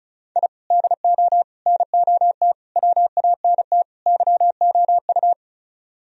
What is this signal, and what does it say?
Morse code, heard as a keyed tone, about 42 wpm, 700 Hz. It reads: I DO NOT WANT YOU
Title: Speed-Racing - Sets of 5 Words 35wpm